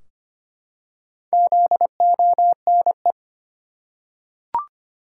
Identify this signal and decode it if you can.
Morse code, heard as a keyed tone, about 25 wpm, 700 Hz. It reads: ZONE
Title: Four Letter Words - 25wpm